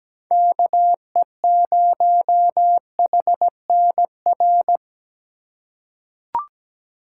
Morse code, heard as a keyed tone, about 17 wpm, 700 Hz. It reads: KE0HNR
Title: USA Call Signs - 6 Characters - 2x Character Spacing - 17wpm